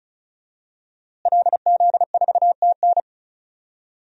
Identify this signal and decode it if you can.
Morse code, heard as a keyed tone, about 36 wpm, 700 Hz. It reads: LZ4TN
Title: DXCC-100 Call Signs - 5 Characters - Repeated 3x - 35wpm